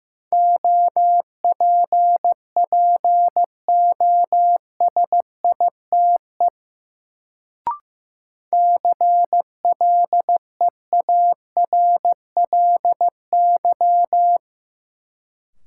Morse code, heard as a keyed tone, about 15 wpm, 700 Hz. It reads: OPPOSITE CLEARLY
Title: Sentences from Top 1300 Words - New Word Drill - 15wpm